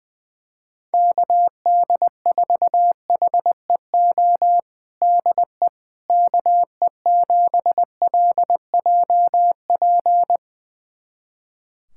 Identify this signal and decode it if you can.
Morse code, heard as a keyed tone, about 20 wpm, 700 Hz. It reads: KD4HEO DE KE7LJP
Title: USA Callsign DE Callsign 20wpm